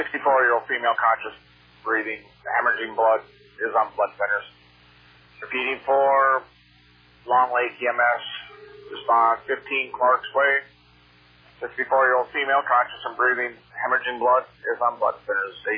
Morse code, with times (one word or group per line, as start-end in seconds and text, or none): none